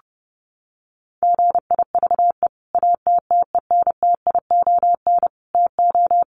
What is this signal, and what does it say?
Morse code, heard as a keyed tone, about 30 wpm, 700 Hz. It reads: GIVE ATTENTION TO
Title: Sentences from Top 1000 Words 30wpm